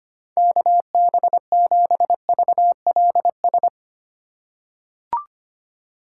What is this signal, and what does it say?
Morse code, heard as a keyed tone, about 25 wpm, 700 Hz. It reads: KB7VLS
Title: USA Call Signs - 6 Characters - 3x Character Spacing - 25wpm